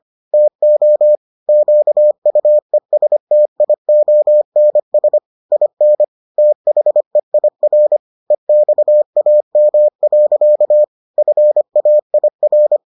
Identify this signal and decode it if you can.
Morse code, heard as a keyed tone, about 25 wpm, 600 Hz. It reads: TO QUESTIONS IN THEIR EXAM. FAIR